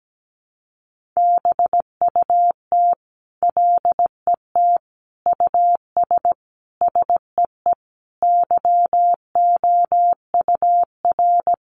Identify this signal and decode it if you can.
Morse code, heard as a keyed tone, about 17 wpm, 700 Hz. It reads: BUT LET US SEE YOUR
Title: Sentences from Top 500 Words 17wpm